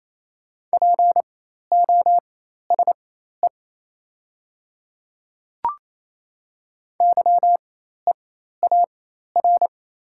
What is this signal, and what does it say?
Morse code, heard as a keyed tone, about 27 wpm, spaced as for 13 wpm, 700 Hz. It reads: POSE YEAR
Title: 4x Character Spacing - Top 500 Words - 28wpm